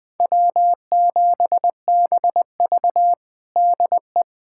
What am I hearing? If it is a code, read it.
Morse code, heard as a keyed tone, about 21 wpm, 700 Hz. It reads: W7BV DE
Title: USA Callsign DE Callsign - 4 Characters - 20wpm